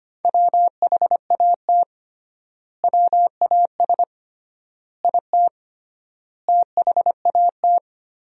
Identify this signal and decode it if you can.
Morse code, heard as a keyed tone, about 25 wpm, 700 Hz. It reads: WHAT WAS IT THAT
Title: 3x Word-Spacing - Sentences from Top 300 Words - 25wpm